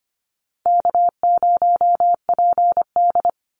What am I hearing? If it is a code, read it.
Morse code, heard as a keyed tone, about 25 wpm, 700 Hz. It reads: K0PD
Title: Call Signs - 25wpm